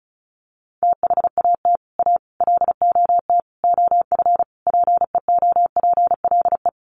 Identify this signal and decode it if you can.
Morse code, heard as a keyed tone, about 34 wpm, 700 Hz. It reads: THAT A LOT OF PEOPLE
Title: Sets of 5 Words 35wpm